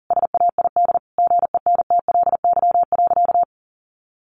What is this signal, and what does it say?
Morse code, heard as a keyed tone, about 40 wpm, 700 Hz. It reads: SAID GENTLY.
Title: Sentences from Top 1500 Words 40wpm